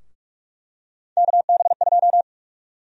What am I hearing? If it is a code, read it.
Morse code, heard as a keyed tone, about 46 wpm, 700 Hz. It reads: KB2
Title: USA Call Sign Prefixes - 45wpm